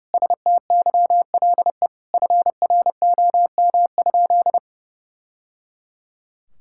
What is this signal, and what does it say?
Morse code, heard as a keyed tone, about 30 wpm, 700 Hz. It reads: STYLE FROM?
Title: Sentences from Top 1000 Words - Review Part 5 - 30wpm